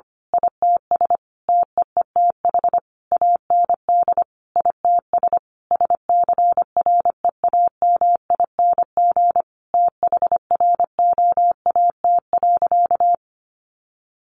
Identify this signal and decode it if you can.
Morse code, heard as a keyed tone, about 25 wpm, 700 Hz. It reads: ITS TEETH AND ITS SCREAMING THROAT.
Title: A Sound Of Thunder 25wpm